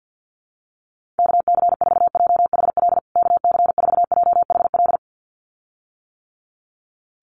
Morse code, heard as a keyed tone, about 50 wpm, 700 Hz. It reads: KC4JHL KC4JHL
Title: USA Call Signs Repeated 2x 50wpm